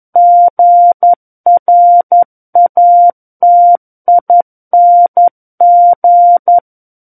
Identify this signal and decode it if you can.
Morse code, heard as a keyed tone, about 11 wpm, 700 Hz. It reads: GRATING
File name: Book One / 02-Chapter 2 - 11 WPM